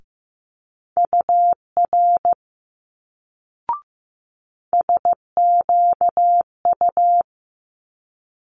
Morse code, heard as a keyed tone, about 15 wpm, 700 Hz. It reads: UR SQU
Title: Multi-letter Phonemes Quick Fire 15wpm